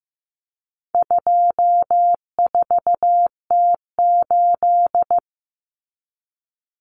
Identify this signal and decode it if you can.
Morse code, heard as a keyed tone, about 15 wpm, 700 Hz. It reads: 24T8
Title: Common Cut Numbers - 4-Digit 15wpm